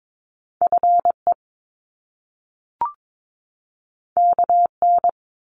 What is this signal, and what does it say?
Morse code, heard as a keyed tone, about 22 wpm, 700 Hz. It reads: FE KN
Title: Top Two Letter Combinations in Words 22wpm